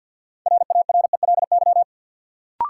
Morse code, heard as a keyed tone, about 52 wpm, 700 Hz. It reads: RARELY
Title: Sentences from Top 1400 Words - New Word Drill Alt - 50wpm